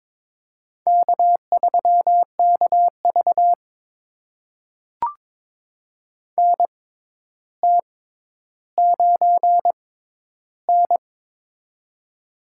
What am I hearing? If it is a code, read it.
Morse code, heard as a keyed tone, about 22 wpm, spaced as for 11 wpm, 700 Hz. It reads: K3KV NT9N
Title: USA Call Signs - 4 Characters - 6x Character Spacing - 22wpm